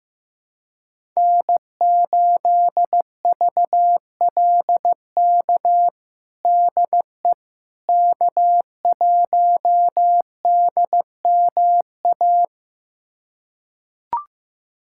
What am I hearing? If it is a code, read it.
Morse code, heard as a keyed tone, about 15 wpm, 700 Hz. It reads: N8VLK DE K1DMA
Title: USA Callsign DE Callsign - 5 Characters - 15wpm